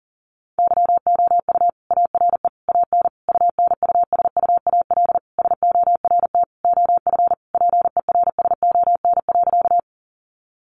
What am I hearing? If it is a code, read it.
Morse code, heard as a keyed tone, about 40 wpm, 700 Hz. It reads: YOU ARE AN UNUSUAL SORT OF PERSON.
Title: Sentences from Top 1400 Words 40wpm